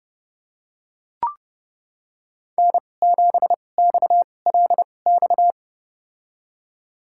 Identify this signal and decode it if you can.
Morse code, heard as a keyed tone, about 30 wpm, spaced as for 22 wpm, 700 Hz. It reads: N7XLX
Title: USA Call Signs - 2x Character Spacing - 30wpm